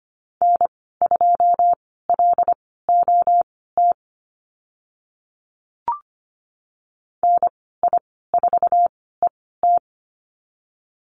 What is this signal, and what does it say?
Morse code, heard as a keyed tone, about 25 wpm, spaced as for 16 wpm, 700 Hz. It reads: N2LOT NI4ET
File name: Lesson 15g - Number 5 - Call Signs 25wpm